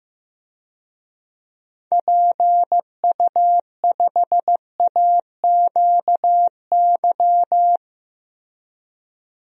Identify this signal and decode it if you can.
Morse code, heard as a keyed tone, about 15 wpm, 700 Hz. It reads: PU5AQY
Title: DXCC-100 Call Signs - 6 Characters - Repeated 2x - 15wpm